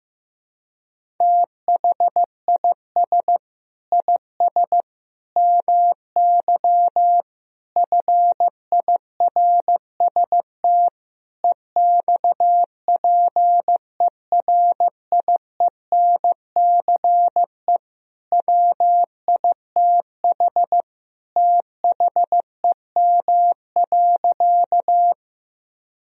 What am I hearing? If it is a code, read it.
Morse code, heard as a keyed tone, about 15 wpm, 700 Hz. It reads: THIS IS MY FIRST EXPERIENCE WITH THEM.
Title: Sentences from Top 1000 Words - Review Part 3 - 15wpm